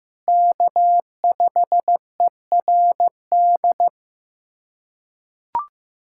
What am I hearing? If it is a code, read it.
Morse code, heard as a keyed tone, about 15 wpm, 700 Hz. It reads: K5ERD
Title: USA Callsign DE Callsign - 5 Characters - 15wpm